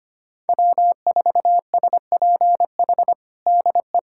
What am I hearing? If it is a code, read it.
Morse code, heard as a keyed tone, about 25 wpm, 700 Hz. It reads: W4SPH DE
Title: USA Callsign DE Callsign - 5 Characters - 25wpm